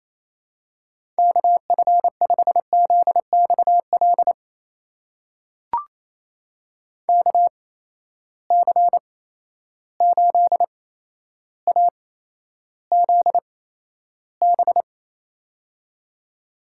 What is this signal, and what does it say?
Morse code, heard as a keyed tone, about 28 wpm, spaced as for 12 wpm, 700 Hz. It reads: KF5ZXL KC8AZB
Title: USA Call Signs - 6 Characters - 8x Character Spacing - 28wpm